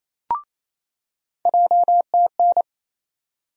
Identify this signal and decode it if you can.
Morse code, heard as a keyed tone, about 28 wpm, 700 Hz. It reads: JTN
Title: USA Call Sign Postfixes - 3 Letters 28wpm